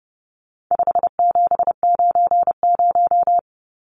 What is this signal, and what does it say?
Morse code, heard as a keyed tone, about 30 wpm, 700 Hz. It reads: 5790
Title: Numbers - 4-Digit 30wpm